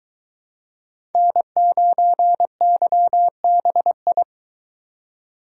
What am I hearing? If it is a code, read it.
Morse code, heard as a keyed tone, about 22 wpm, 700 Hz. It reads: N9YBI